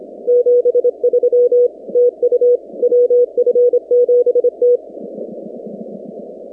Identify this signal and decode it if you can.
Morse code, heard as a keyed tone, about 26 wpm, 490 Hz. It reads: 73 TU WF7T
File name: 2019-cw-wpx-kt5j-k5pi